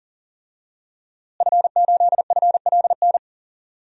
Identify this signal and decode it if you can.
Morse code, heard as a keyed tone, about 39 wpm, 700 Hz. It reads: F8FLN
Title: DXCC-100 Call Signs - 5 Characters - Repeated 2x - 40wpm